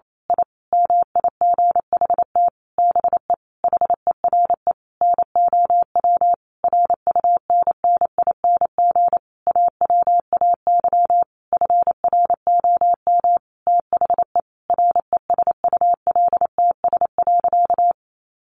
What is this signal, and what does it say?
Morse code, heard as a keyed tone, about 28 wpm, 700 Hz. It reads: I MIGHT BE HERE NOW RUNNING AWAY FROM THE RESULTS.
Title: A Sound Of Thunder 28wpm